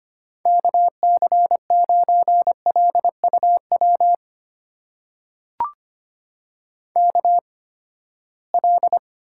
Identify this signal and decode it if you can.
Morse code, heard as a keyed tone, about 25 wpm, spaced as for 17 wpm, 700 Hz. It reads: KC9LUW KL
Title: USA Call Signs - 6 Characters - 8x Character Spacing - 25wpm